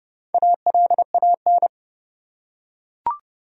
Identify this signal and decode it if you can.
Morse code, heard as a keyed tone, about 30 wpm, 700 Hz. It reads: ALAN
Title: USA Common Names - Max 4 Letters - 30wpm